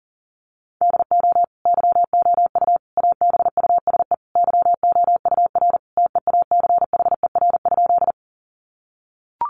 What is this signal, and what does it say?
Morse code, heard as a keyed tone, about 40 wpm, 700 Hz. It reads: DO YOU ABUSE YOUR TEACHER?
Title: Sentences from Top 1300 Words 40wpm